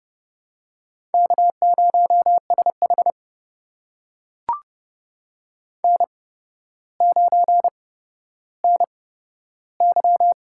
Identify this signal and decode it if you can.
Morse code, heard as a keyed tone, about 30 wpm, spaced as for 13 wpm, 700 Hz. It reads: K0SH N9NY
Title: USA Call Signs - 4 Characters - 8x Character Spacing - 30wpm